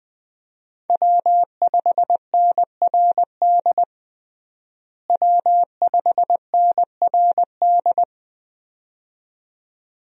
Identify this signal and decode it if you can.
Morse code, heard as a keyed tone, about 21 wpm, 700 Hz. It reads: W5NRD W5NRD
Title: USA Call Signs - 5 Characters - Repeated 2x - 20wpm